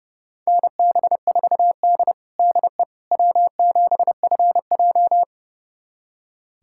Morse code, heard as a keyed tone, about 30 wpm, 700 Hz. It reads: NB4D DE W7FJ
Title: USA Callsign DE Callsign - 4 Characters - 30wpm